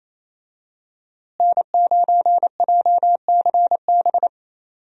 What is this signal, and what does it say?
Morse code, heard as a keyed tone, about 28 wpm, 700 Hz. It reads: N9JCB
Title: USA Call Signs - 5 Characters - Repeated 3x - 28wpm